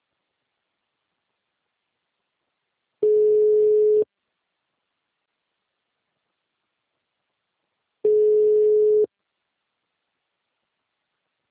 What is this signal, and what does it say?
Ringback tone (Europe, 425 Hz).